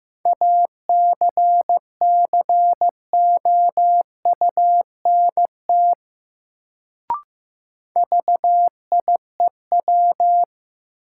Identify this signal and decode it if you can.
Morse code, heard as a keyed tone, about 15 wpm, 700 Hz. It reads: ACCOUNT VIEW
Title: Sentences from Top 900 Words - New Word Drill - 15wpm